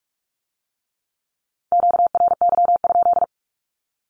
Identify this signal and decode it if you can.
Morse code, heard as a keyed tone, about 46 wpm, 700 Hz. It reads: QRY?
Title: Q-Codes with Spoken Meaning - 45wpm